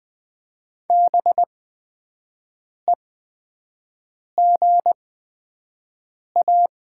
Morse code, heard as a keyed tone, about 20 wpm, spaced as for 5 wpm, 700 Hz. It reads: BEGA
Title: 8x Character Spacing - Top 300 Words - 20wpm